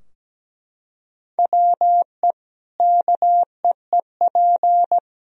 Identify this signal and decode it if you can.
Morse code, heard as a keyed tone, about 17 wpm, 700 Hz. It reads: WE KEEP
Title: Sentences from Top 1000 Words - Review Part 4 - 17wpm